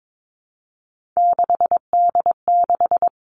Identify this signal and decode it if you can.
Morse code, heard as a keyed tone, about 22 wpm, 700 Hz. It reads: 6D6
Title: Three Challenging Letter-Number Combinations 22wpm